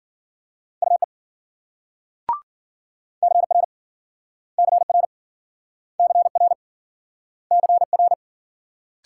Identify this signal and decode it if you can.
Morse code, heard as a keyed tone, about 53 wpm, 700 Hz. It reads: UE CR CR CR CR